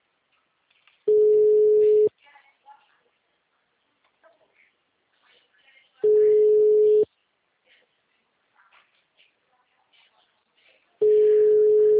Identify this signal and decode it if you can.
Ringback tone (Europe, 425 Hz), with speech.
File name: out- -2010- - - .72